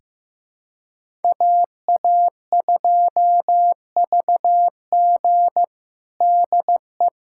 Morse code, heard as a keyed tone, about 15 wpm, 700 Hz. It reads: AA2VG DE
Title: USA Callsign DE Callsign - 5 Characters - 15wpm